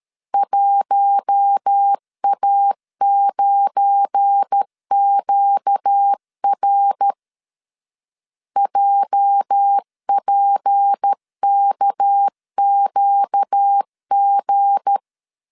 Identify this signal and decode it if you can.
Morse code, heard as a keyed tone, about 13 wpm, 800 Hz. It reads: 1A9QR JPKQG
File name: CW / DET / 1 9 1500C S A 12wpm 3800